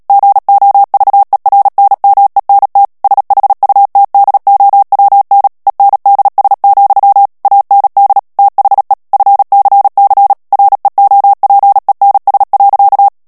Morse code, heard as a keyed tone, about 37 wpm, 800 Hz. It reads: GOVERNMENT SHUTDOWN ENDS, AND THE FCC REOPENS.
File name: morse mp3 / ANGLAIS texte clair 3 vitesse 35 mots minute